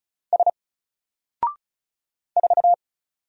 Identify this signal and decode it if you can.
Morse code, heard as a keyed tone, about 34 wpm, 700 Hz. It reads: S 4
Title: Single Character (Letter:Number:Punctuation:Prosign) Alt 35wpm